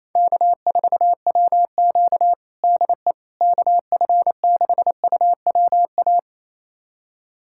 Morse code, heard as a keyed tone, about 28 wpm, 700 Hz. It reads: K4WQ DE KF6UWA
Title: USA Callsign DE Callsign 28wpm